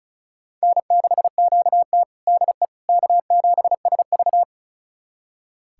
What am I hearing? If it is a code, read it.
Morse code, heard as a keyed tone, about 36 wpm, 700 Hz. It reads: N6QT DE K7SV
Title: USA Callsign DE Callsign - 4 Characters - 35wpm